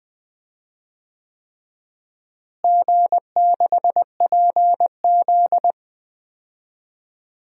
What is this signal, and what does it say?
Morse code, heard as a keyed tone, about 20 wpm, 700 Hz. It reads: G6PZ
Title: DXCC-100 Call Signs - 4 Characters - Repeated 3x - 20wpm